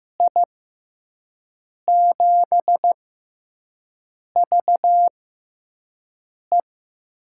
Morse code, heard as a keyed tone, about 15 wpm, spaced as for 5 wpm, 700 Hz. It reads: I7VE
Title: USA Call Signs - 6 Characters - 6x Character Spacing - 15wpm